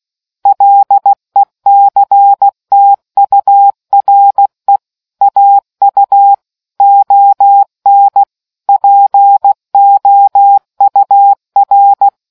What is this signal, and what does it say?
Morse code, heard as a keyed tone, about 16 wpm, 800 Hz. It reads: LECTURE AU ON POUR